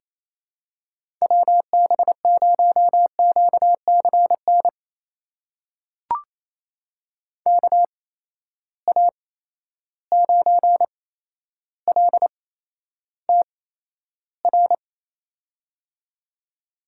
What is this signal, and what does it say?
Morse code, heard as a keyed tone, about 28 wpm, spaced as for 12 wpm, 700 Hz. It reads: WB0QCN KA9LTR
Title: USA Call Signs - 6 Characters - 8x Character Spacing - 28wpm